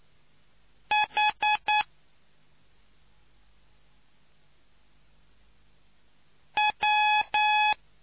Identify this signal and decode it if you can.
Morse code, heard as a keyed tone, about 9 wpm, 850 Hz. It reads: H W